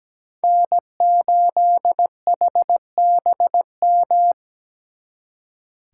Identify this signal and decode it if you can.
Morse code, heard as a keyed tone, about 18 wpm, 700 Hz. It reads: N8HBM
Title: Speed Racing - USA 1x3 Call Signs 17wpm